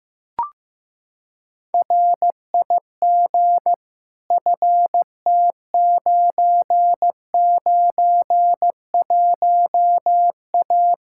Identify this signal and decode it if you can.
Morse code, heard as a keyed tone, about 15 wpm, 700 Hz. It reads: RIG FT991A
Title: K1USN SST QSO Practice 15wpm